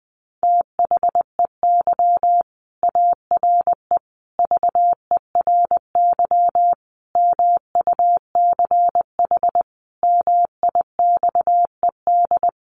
Morse code, heard as a keyed tone, about 20 wpm, 700 Hz. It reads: THEY ARE VERY MUCH MIXED
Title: Sentences from Top 1500 Words 20wpm